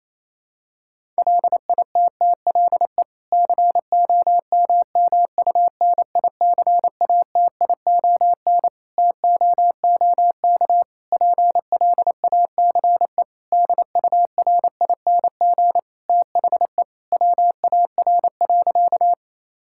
Morse code, heard as a keyed tone, about 28 wpm, 700 Hz. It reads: LITTLE COMMUNICATION TOOK PLACE DURING THE WAR.
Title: Sentences from Top 1000 Words - Review Part 6 - 28wpm